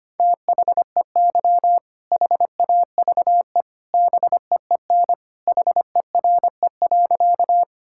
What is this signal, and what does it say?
Morse code, heard as a keyed tone, about 25 wpm, 700 Hz. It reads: THEY HAVE BEEN HERE.
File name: practice / 25wpm / Sentences with 2-Word Phrases - 25wpm